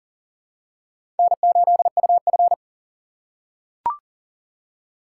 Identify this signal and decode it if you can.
Morse code, heard as a keyed tone, about 40 wpm, 700 Hz. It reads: N8UF
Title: USA Call Signs - 4 Characters - 3x Character Spacing - 40wpm